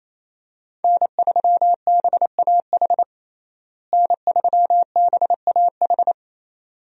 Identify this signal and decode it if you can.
Morse code, heard as a keyed tone, about 27 wpm, 700 Hz. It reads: N3BAH N3BAH
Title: USA Call Signs - 5 Characters - Repeated 2x - 28wpm